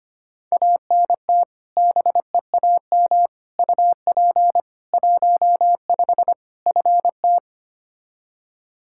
Morse code, heard as a keyed tone, about 25 wpm, 700 Hz. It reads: ANT BEAM UP 15 FT
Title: Practice QSO - Exchange 2 - Info - 25wpm